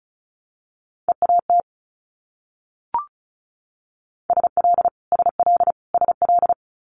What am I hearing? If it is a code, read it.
Morse code, heard as a keyed tone, about 36 wpm, 700 Hz. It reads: EAT SL SL SL
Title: Multi-letter Phonemes Repeated 3x 35wpm